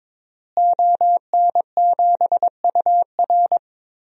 Morse code, heard as a keyed tone, about 22 wpm, 700 Hz. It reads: ON7UR
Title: DXCC-100 Call Signs - 5 Characters - Repeated 3x - 22wpm